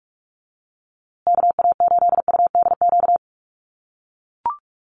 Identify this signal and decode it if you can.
Morse code, heard as a keyed tone, about 44 wpm, 700 Hz. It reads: KA8UDQ
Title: USA Call Signs Repeated 3x 45wpm